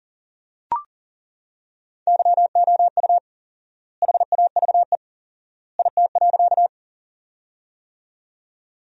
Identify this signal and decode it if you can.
Morse code, heard as a keyed tone, about 37 wpm, 700 Hz. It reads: YOU HAVE IT.
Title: 4x Word-Spacing - Sentences from Top 100 Words - 40wpm